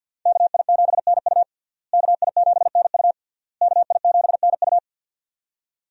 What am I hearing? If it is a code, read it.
Morse code, heard as a keyed tone, about 52 wpm, 700 Hz. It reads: KI7NU KI7NU KI7NU
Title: USA Call Signs - 5 Characters - Repeated 3x - 50wpm